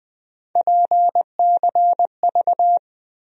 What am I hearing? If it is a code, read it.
Morse code, heard as a keyed tone, about 20 wpm, 700 Hz. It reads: PCV
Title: USA Call Sign Postfixes - 3 Letters 20wpm